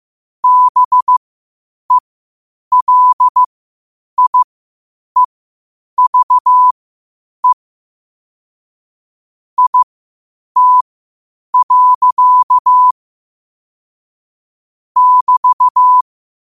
Morse code, heard as a keyed tone, about 15 wpm, spaced as for 8 wpm, 1000 Hz. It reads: BELIEVE IT. <BT>
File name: qotdfiles / 8wpm